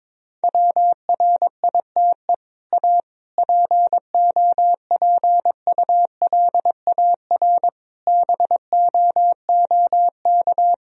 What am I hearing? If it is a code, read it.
Morse code, heard as a keyed tone, about 22 wpm, 700 Hz. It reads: WRITE A POPULAR BOOK
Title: Sentences from Top 1000 Words - Review Part 3 - 22wpm